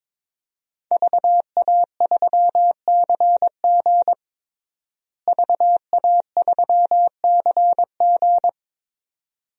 Morse code, heard as a keyed tone, about 22 wpm, 700 Hz. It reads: VA3CG VA3CG
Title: DXCC-100 Call Signs - 5 Characters - Repeated 2x - 22wpm